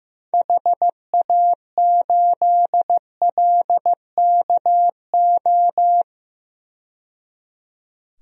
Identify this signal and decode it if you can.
Morse code, heard as a keyed tone, about 15 wpm, 700 Hz. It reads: HA8LKO
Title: DXCC-100 Call Signs - 6 Characters - 15wpm